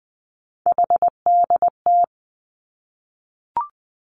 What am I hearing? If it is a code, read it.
Morse code, heard as a keyed tone, about 20 wpm, 700 Hz. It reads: HDT
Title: Three Challenging Letter-Number Combinations 20wpm